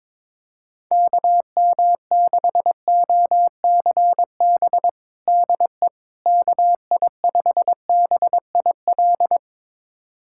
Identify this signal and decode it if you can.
Morse code, heard as a keyed tone, about 23 wpm, 700 Hz. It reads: KM6OCB DE KI5BIL
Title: USA Callsign DE Callsign - 6 Characters - 22wpm